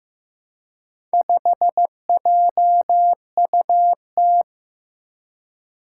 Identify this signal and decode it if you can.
Morse code, heard as a keyed tone, about 15 wpm, 700 Hz. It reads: 5JUT
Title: Four Challenging Letter-Number Combinations 15wpm